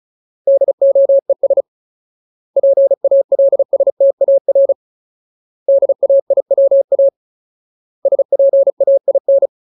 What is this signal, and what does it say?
Morse code, heard as a keyed tone, about 34 wpm, 550 Hz. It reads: DOES PALSTAR DAIWA SPAIN
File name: Ultimate Word List / 35WPM Ultimate